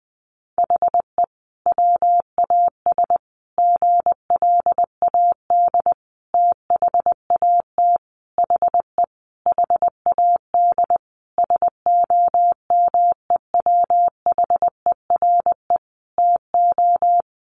Morse code, heard as a keyed tone, about 20 wpm, 700 Hz. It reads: HE WAS GLAD THAT HE HAD SOMEWHERE TO
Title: Sentences from Top 1400 Words 20wpm